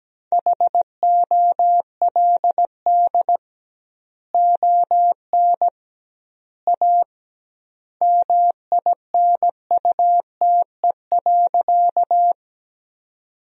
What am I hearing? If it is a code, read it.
Morse code, heard as a keyed tone, about 17 wpm, 700 Hz. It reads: HOLD ON A MINUTE.
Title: 2x Word-Spacing - Sentences from Top 500 Words - 17wpm